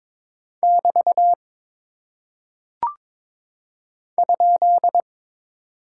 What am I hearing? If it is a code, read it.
Morse code, heard as a keyed tone, about 22 wpm, 700 Hz. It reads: <BT> ?
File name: practice / 22wpm / Single Punctuation-Prosign - 22wpm